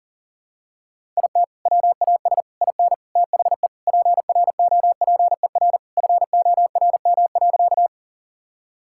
Morse code, heard as a keyed tone, about 40 wpm, 700 Hz. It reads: IT WAS IN THE PROPER FORM.
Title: Sentences from Top 1000 Words - Review Part 5 - 40wpm